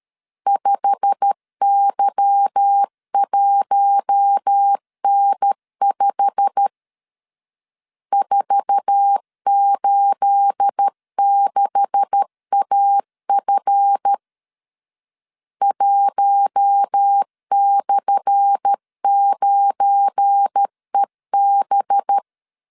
Morse code, heard as a keyed tone, about 13 wpm, 800 Hz. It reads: 5Y1N5 486AF 1/9EB
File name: LEZIONI / 10 SIGNS / 10A SLASH 12wpm 2314